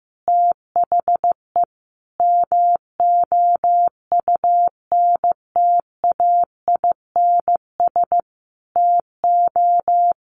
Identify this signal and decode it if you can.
Morse code, heard as a keyed tone, about 15 wpm, 700 Hz. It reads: THE MOUNTAINS TO
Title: A Sound Of Thunder 15wpm